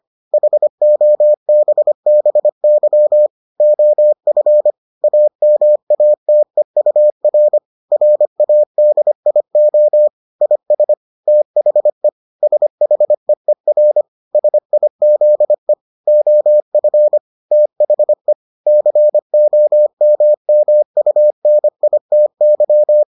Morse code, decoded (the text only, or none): HOBBY OF AMATEUR RADIO IS THE SHEER SIZE OF THE COMMUNITY